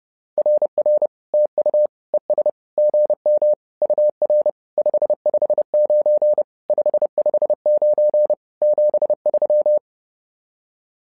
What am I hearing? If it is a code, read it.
Morse code, heard as a keyed tone, about 30 wpm, 600 Hz. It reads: RR TU ES GM UR 559 559 73
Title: SOTA QSO Practice 30wpm